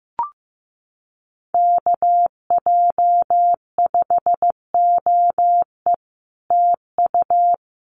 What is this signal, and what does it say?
Morse code, heard as a keyed tone, about 15 wpm, 700 Hz. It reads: KJ5OE TU
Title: Common QSO Phrases Encore 15wpm